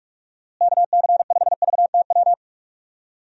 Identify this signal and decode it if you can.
Morse code, heard as a keyed tone, about 46 wpm, 700 Hz. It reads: KC5VTW
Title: USA Call Signs - 6 Characters - Repeated 2x - 45wpm